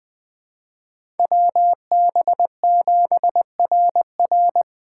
Morse code, heard as a keyed tone, about 24 wpm, 700 Hz. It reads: WB7RR
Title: Speed Racing - USA 2x2 Call Signs 20wpm